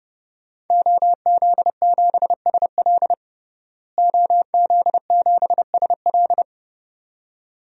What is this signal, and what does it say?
Morse code, heard as a keyed tone, about 32 wpm, 700 Hz. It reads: OZ7SL OZ7SL
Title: DXCC-100 Call Signs - 5 Characters - Repeated 2x - 30wpm